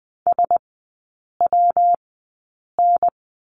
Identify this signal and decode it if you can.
Morse code, heard as a keyed tone, about 20 wpm, spaced as for 8 wpm, 700 Hz. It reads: SWN
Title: 114 Lesson 19e - Letter w - Three Characters 20wpm